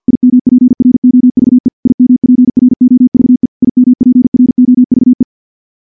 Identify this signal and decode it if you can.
Morse code, heard as a keyed tone, about 50 wpm, 250 Hz. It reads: IMPROVE IMPROVE IMPROVE
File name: morse / morse-training-50WPM